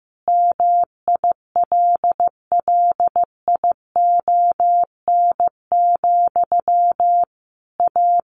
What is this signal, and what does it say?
Morse code, heard as a keyed tone, about 15 wpm, 700 Hz. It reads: MILLION, A
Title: A Sound Of Thunder 15wpm